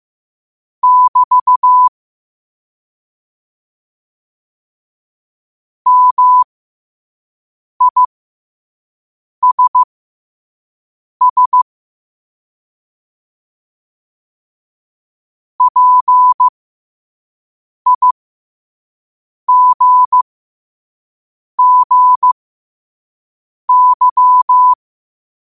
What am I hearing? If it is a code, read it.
Morse code, heard as a keyed tone, about 15 wpm, spaced as for 5 wpm, 1000 Hz. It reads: <BT> MISS PIGGY
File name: qotdfiles / 5wpm